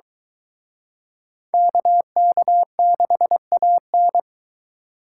Morse code, heard as a keyed tone, about 23 wpm, 700 Hz. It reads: KK6AN